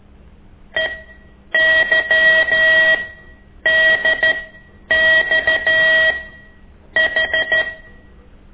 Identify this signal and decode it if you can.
Morse code, heard as a keyed tone, about 13 wpm, spaced as for 8 wpm, 1900 Hz. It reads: EYDXH
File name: ik1weg2 / CORSI CW